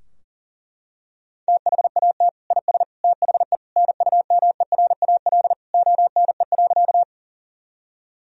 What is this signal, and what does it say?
Morse code, heard as a keyed tone, about 40 wpm, 700 Hz. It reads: THAT IS THE NUMERAL ONE.
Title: Sentences from Top 1000 Words - Review Part 2 - 40wpm